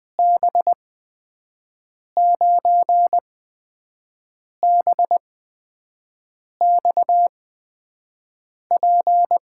Morse code, heard as a keyed tone, about 20 wpm, spaced as for 5 wpm, 700 Hz. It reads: B9BXP
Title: USA Call Signs - 8x Character Spacing - 20wpm